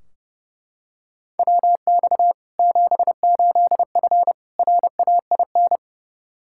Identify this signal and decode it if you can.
Morse code, heard as a keyed tone, about 30 wpm, 700 Hz. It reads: WX 78F RAIN
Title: K1USN SST QSO Practice 30wpm